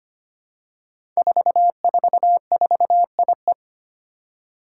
Morse code, heard as a keyed tone, about 25 wpm, 700 Hz. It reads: 444IE
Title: Five Challenging Letter-Number Combinations 25wpm